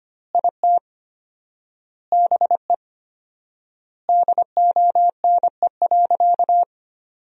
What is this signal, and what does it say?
Morse code, heard as a keyed tone, about 25 wpm, 700 Hz. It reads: IT BE DONE.
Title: 4x Word-Spacing - Sentences from Top 500 Words - 25wpm